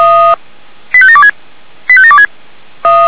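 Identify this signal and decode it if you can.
Five-tone selective call (ZVEI), heard as quick runs of five tones, tones of about 70 ms. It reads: 85726 85726
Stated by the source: ZVEI five-tone selective call received by SDR.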